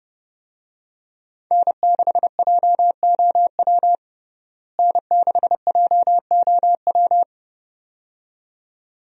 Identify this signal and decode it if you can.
Morse code, heard as a keyed tone, about 32 wpm, 700 Hz. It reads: N6JOW N6JOW
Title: USA Call Signs - 5 Characters - Repeated 2x - 30wpm